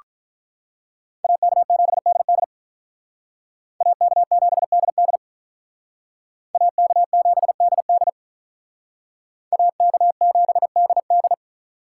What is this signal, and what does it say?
Morse code, heard as a keyed tone, about 45 wpm, 700 Hz. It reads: AK7DD AK7DD AK7DD AK7DD